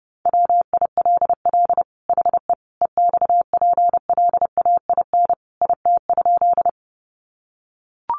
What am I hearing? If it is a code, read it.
Morse code, heard as a keyed tone, about 30 wpm, 700 Hz. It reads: WILL HE EXPLAIN IT?
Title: Sentences from Top 1100 Words 30wpm